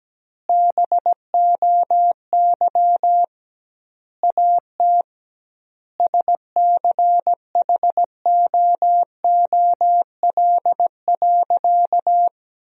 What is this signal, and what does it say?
Morse code, heard as a keyed tone, about 17 wpm, 700 Hz. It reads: BOY AT SCHOOL.
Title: 2x Word-Spacing - Sentences from Top 300 Words - 17wpm